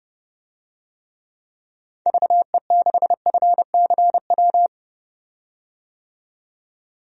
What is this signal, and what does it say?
Morse code, heard as a keyed tone, about 32 wpm, 700 Hz. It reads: VE6FCW
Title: DXCC-100 Call Signs - 6 Characters - 30wpm